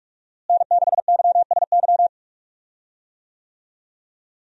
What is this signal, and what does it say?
Morse code, heard as a keyed tone, about 46 wpm, 700 Hz. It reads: N6YSY
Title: USA Call Signs - 5 Characters - Repeated 2x - 45wpm